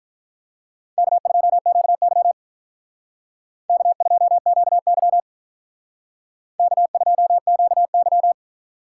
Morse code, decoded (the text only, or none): K2QY K2QY K2QY